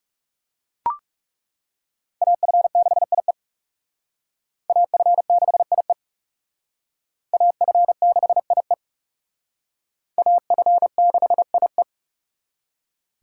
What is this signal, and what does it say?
Morse code, heard as a keyed tone, about 37 wpm, 700 Hz. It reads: AF6IE AF6IE AF6IE AF6IE
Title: Speed Racing - USA 2x2 Call Signs 30wpm